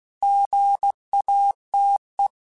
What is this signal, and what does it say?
Morse code, heard as a keyed tone, about 16 wpm, 800 Hz. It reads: GATE
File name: Technical Subjects / Technical Subjects-mp3s / Gates Intro @15